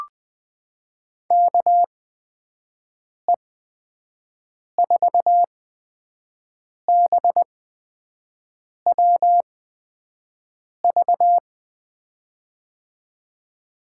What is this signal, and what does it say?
Morse code, heard as a keyed tone, about 20 wpm, spaced as for 5 wpm, 700 Hz. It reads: KE4BWV
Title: USA Call Signs - 6 Characters - 8x Character Spacing - 20wpm